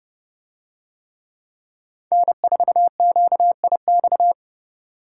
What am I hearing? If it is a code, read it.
Morse code, heard as a keyed tone, about 31 wpm, 700 Hz. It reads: N4QIX